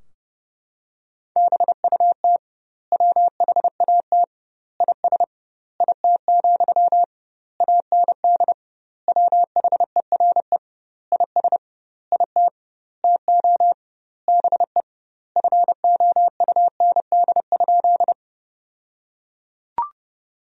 Morse code, heard as a keyed tone, about 30 wpm, 700 Hz. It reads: BUT WHAT IS IT, AND WHERE IS IT TO BE FOUND?
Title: 2x Word-Spacing - Sentences from Top 300 Words - 30wpm